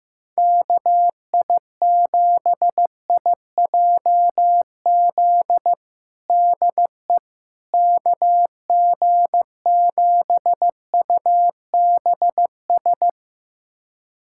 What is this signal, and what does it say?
Morse code, heard as a keyed tone, about 16 wpm, 700 Hz. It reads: KI7IJZ DE KG7UBS
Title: USA Callsign DE Callsign 15wpm